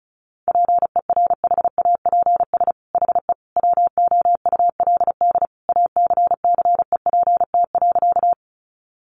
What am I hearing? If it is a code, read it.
Morse code, heard as a keyed tone, about 34 wpm, 700 Hz. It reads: PERHAPS HE WOULD ACCEPT.
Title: Sentences from Top 1300 Words 35wpm